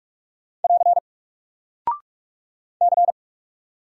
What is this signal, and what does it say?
Morse code, heard as a keyed tone, about 44 wpm, 700 Hz. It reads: <AR> C
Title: Single Character (Letter:Number:Punctuation:Prosign) 45wpm